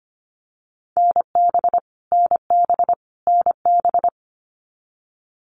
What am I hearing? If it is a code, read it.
Morse code, heard as a keyed tone, about 25 wpm, 700 Hz. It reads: NB NB NB
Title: Canadian Provinces and Territories - Repeated 3x - 25wpm